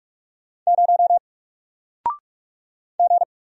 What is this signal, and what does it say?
Morse code, heard as a keyed tone, about 45 wpm, 700 Hz. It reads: 0 G
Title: Single Letter-Number Alt 45wpm